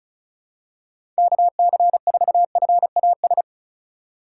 Morse code, heard as a keyed tone, about 36 wpm, 700 Hz. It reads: KC4FAS
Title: USA Call Signs - 6 Characters - 2x Character Spacing - 35wpm